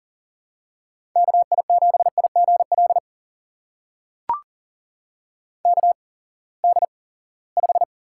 Morse code, heard as a keyed tone, about 40 wpm, spaced as for 23 wpm, 700 Hz. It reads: KI7IGL KD5
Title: USA Call Signs - 6 Characters - 8x Character Spacing - 40wpm